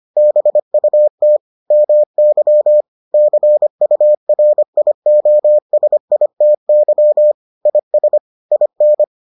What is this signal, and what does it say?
Morse code, heard as a keyed tone, about 25 wpm, 600 Hz. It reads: BUT MY CURIOSITY IS IN